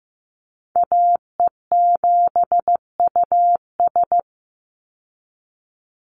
Morse code, heard as a keyed tone, about 15 wpm, 700 Hz. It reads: AE7US
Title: Call Signs Repeated 3x - 15wpm